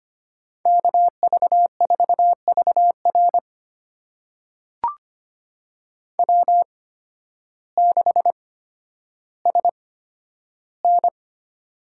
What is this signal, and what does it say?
Morse code, heard as a keyed tone, about 25 wpm, spaced as for 12 wpm, 700 Hz. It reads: KV4VR W6SN
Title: USA Call Signs - 8x Character Spacing - 25wpm